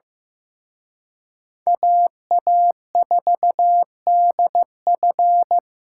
Morse code, heard as a keyed tone, about 17 wpm, 700 Hz. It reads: AA4DF
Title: Speed Racing - USA 2x2 Call Signs 15wpm